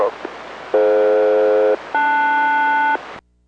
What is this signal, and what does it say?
Aircraft SELCAL, heard as two pairs of tones, with speech. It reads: DF-LS